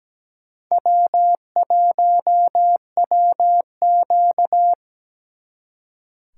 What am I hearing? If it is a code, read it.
Morse code, heard as a keyed tone, about 20 wpm, 700 Hz. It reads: W1WQ
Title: Speed Racing - USA 1x2 Call Signs 17wpm